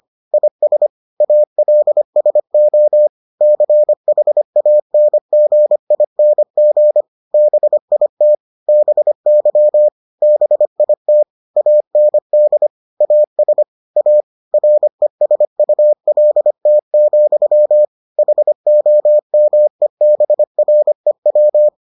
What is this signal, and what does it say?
Morse code, heard as a keyed tone, about 25 wpm, 600 Hz. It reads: IS ALSO CHANGING BIT BY BIT AND AS A RESULT, HOMEBREW